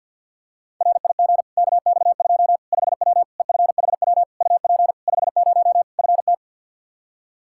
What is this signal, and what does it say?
Morse code, heard as a keyed tone, about 53 wpm, 700 Hz. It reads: RIG KX2 5W EFHW UP 50 FT
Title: Practice QSO - Exchange 2 - Info - 50wpm